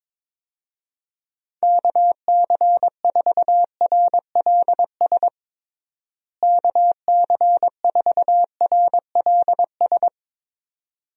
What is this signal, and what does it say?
Morse code, heard as a keyed tone, about 22 wpm, 700 Hz. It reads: KC4RLS KC4RLS
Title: USA Call Signs - 6 Characters - Repeated 2x - 22wpm